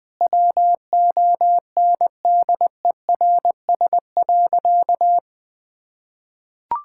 Morse code, heard as a keyed tone, about 20 wpm, 700 Hz. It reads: WONDERS.
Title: Sentences with 4-Word Phrases - 20wpm